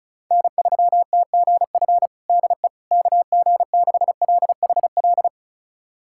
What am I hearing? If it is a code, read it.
Morse code, heard as a keyed tone, about 34 wpm, 700 Hz. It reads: N3TGF DE KG6LHL
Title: USA Callsign DE Callsign 35wpm